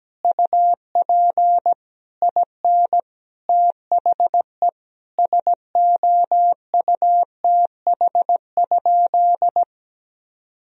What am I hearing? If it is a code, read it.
Morse code, heard as a keyed tone, about 17 wpm, 700 Hz. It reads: UP IN THE SOUTH?
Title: Sentences from Top 1000 Words - Review Part 2 - 17wpm